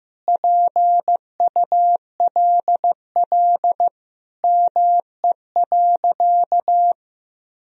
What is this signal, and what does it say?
Morse code, heard as a keyed tone, about 15 wpm, 700 Hz. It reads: PULL ME.
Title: Sentences from Top 1000 Words - Review Part 2 - 15wpm